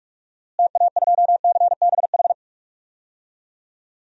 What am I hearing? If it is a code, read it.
Morse code, heard as a keyed tone, about 45 wpm, 700 Hz. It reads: TA2CBH
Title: DXCC-100 Call Signs - 6 Characters - Repeated 2x - 45wpm